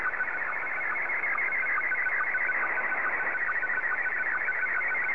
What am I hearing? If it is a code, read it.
SSTV (slow-scan television), Scottie 1.